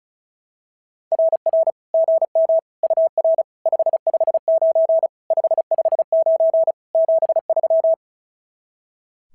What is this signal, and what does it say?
Morse code, heard as a keyed tone, about 33 wpm, 660 Hz. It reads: RR GM UR 559 559 73
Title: SOTA QSO Practice 35wpm